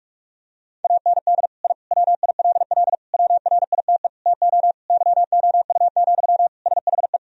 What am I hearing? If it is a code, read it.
Morse code, heard as a keyed tone, about 41 wpm, 700 Hz. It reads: AND I WILL WRITE TO YOU, SHE
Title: Sentences from Top 1000 Words - Review Part 1 - 45wpm